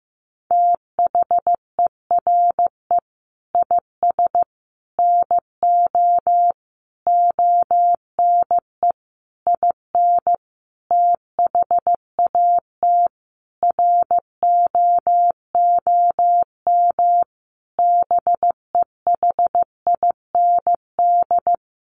Morse code, heard as a keyed tone, about 15 wpm, 700 Hz. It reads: THERE IS NO ONE IN THAT ROOM BEHIND
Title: Sentences from Top 500 Words 15wpm